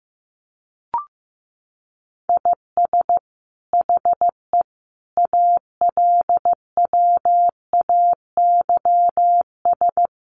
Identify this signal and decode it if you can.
Morse code, heard as a keyed tone, about 14 wpm, 700 Hz. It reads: IS HE ALWAYS
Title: Sentences from Top 1400 Words 15wpm